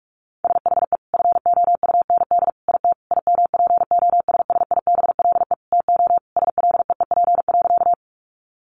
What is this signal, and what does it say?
Morse code, heard as a keyed tone, about 44 wpm, 700 Hz. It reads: SHE FOUND IT IMPOSSIBLE TO SLEEP.
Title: Sentences from Top 1100 Words 45wpm